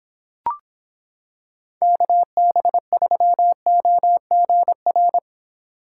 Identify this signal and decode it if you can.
Morse code, heard as a keyed tone, about 26 wpm, 700 Hz. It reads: KB3OGR